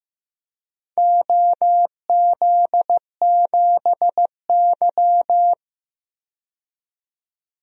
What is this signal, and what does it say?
Morse code, heard as a keyed tone, about 15 wpm, 700 Hz. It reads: OZ7Y
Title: DXCC-100 Call Signs - 4 Characters - 15wpm